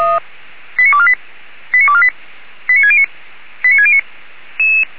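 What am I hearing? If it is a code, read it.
Five-tone selective call (ZVEI), heard as quick runs of five tones, tones of about 70 ms. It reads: 89248 89248 89609 89609
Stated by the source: ZVEI five-tone call 89349 received on an SDR.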